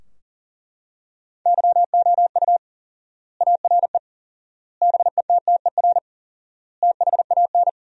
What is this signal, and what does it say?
Morse code, heard as a keyed tone, about 40 wpm, 700 Hz. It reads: YOU ARE BETTER THAN
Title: 4x Word-Spacing - Sentences from Top 400 Words - 40wpm